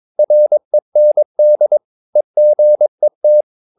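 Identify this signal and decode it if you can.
Morse code, heard as a keyed tone, about 22 wpm, 600 Hz. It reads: REND EGET
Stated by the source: Morse practice of short Danish words at 22 wpm, from CW med Gnister.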